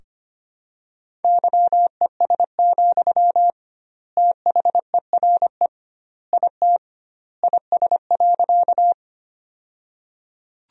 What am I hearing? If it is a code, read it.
Morse code, heard as a keyed tone, about 25 wpm, 700 Hz. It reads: YES, THERE IT IS.
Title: 2x Word-Spacing - Sentences from Top 500 Words - 25wpm